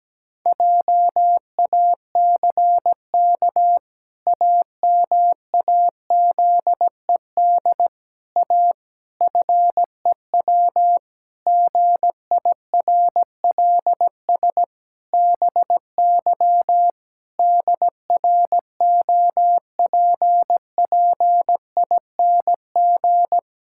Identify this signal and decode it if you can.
Morse code, heard as a keyed tone, about 17 wpm, 700 Hz. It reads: JACK AMAZED A FEW GIRLS BY DROPPING
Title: Pangram Challenge v2 - 17wpm